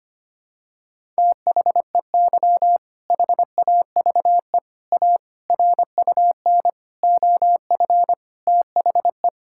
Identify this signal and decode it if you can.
Morse code, heard as a keyed tone, about 25 wpm, 700 Hz. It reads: THEY HAVE A RUN OF THE
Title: Sentences from Top 1000 Words - Review Part 5 - 25wpm